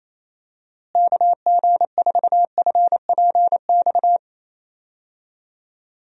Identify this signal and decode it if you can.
Morse code, heard as a keyed tone, about 28 wpm, 700 Hz. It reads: KG4FPX
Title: USA Call Signs - 6 Characters - Repeated 3x - 28wpm